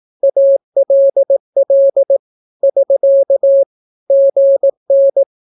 Morse code, heard as a keyed tone, about 18 wpm, 550 Hz. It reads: ALL <SK> GN
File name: naukapodstaw / skroty / skroty